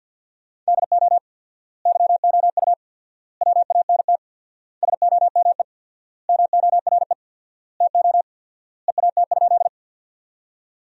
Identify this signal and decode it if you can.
Morse code, heard as a keyed tone, about 50 wpm, 700 Hz. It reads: DO YOU WANT SOME MORE TO EAT?
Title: 4x Word-Spacing - Sentences from Top 300 Words - 50wpm